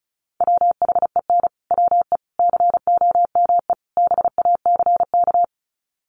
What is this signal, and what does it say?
Morse code, heard as a keyed tone, about 34 wpm, 700 Hz. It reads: WHEN WE COME BACK
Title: Sets of 4 Words 35wpm